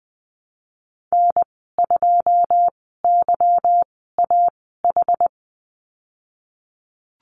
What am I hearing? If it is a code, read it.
Morse code, heard as a keyed tone, about 19 wpm, spaced as for 14 wpm, 700 Hz. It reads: N2YAH
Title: 179 Lesson 29g - Slash - Call Signs 20wpm